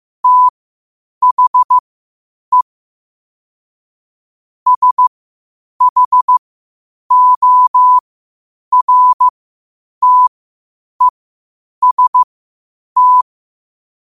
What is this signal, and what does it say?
Morse code, heard as a keyed tone, about 15 wpm, spaced as for 8 wpm, 1000 Hz. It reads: THE SHORTEST
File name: qotdfiles / 8wpm